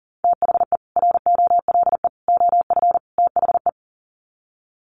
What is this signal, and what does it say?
Morse code, heard as a keyed tone, about 40 wpm, 700 Hz. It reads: THE ROLE OF THE
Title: Sets of 4 Words 40wpm